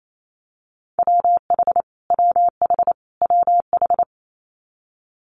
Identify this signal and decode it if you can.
Morse code, heard as a keyed tone, about 28 wpm, 700 Hz. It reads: WH WH WH
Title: Multi-letter Phonemes Repeated 3x 28wpm